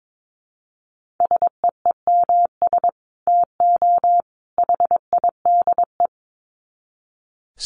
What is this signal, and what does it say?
Morse code, heard as a keyed tone, about 22 wpm, 700 Hz. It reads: SEEMS TO HIDE
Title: Dit Challenge 22wpm